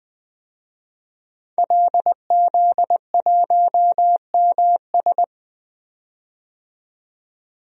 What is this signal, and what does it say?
Morse code, heard as a keyed tone, about 21 wpm, 700 Hz. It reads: LZ1MS
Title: DXCC-100 Call Signs - 5 Characters - 20wpm